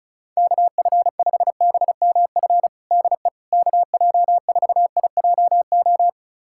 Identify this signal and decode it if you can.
Morse code, heard as a keyed tone, about 34 wpm, 700 Hz. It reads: KF5BMF DE KJ4IJO
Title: USA Callsign DE Callsign - 6 Characters - 35wpm